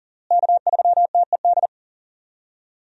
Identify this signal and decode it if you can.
Morse code, heard as a keyed tone, about 40 wpm, 700 Hz. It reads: K3TED
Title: USA Call Signs - 5 Characters - Repeated 3x - 40wpm